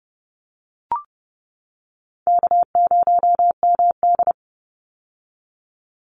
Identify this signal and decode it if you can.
Morse code, heard as a keyed tone, about 30 wpm, 700 Hz. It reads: K0MD
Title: Easy Call Signs - 30wpm